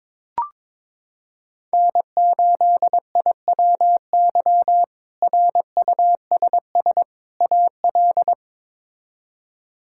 Morse code, heard as a keyed tone, about 21 wpm, 700 Hz. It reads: N8IWY RUSS AL
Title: K1USN SST QSO Practice - 22wpm